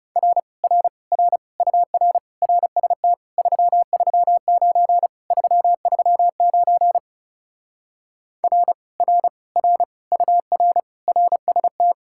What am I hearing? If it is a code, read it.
Morse code, heard as a keyed tone, about 40 wpm, 700 Hz. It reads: R R R UR RST 339 339 R R R UR RST